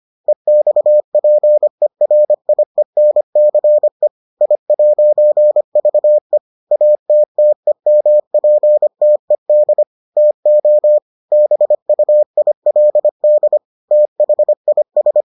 Morse code, heard as a keyed tone, about 25 wpm, 600 Hz. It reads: EXPERIENCE I'VE ATTEMPTED TO BUILD THIS